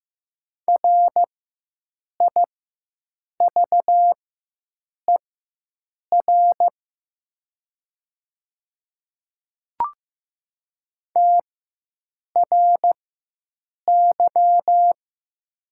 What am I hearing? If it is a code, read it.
Morse code, heard as a keyed tone, about 15 wpm, spaced as for 7 wpm, 700 Hz. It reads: RIVER TRY
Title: 4x Character Spacing - Top 300 Words - 15wpm